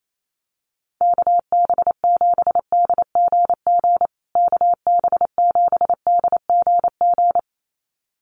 Morse code, heard as a keyed tone, about 28 wpm, 700 Hz. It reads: KB7DGG KB7DGG
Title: USA Call Signs Repeated 2x 28wpm